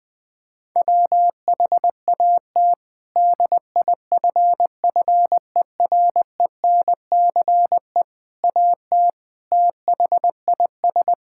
Morse code, heard as a keyed tone, about 20 wpm, 700 Hz. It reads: WHAT DIFFERENCE AT THIS
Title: Sets of 8 Words - 20wpm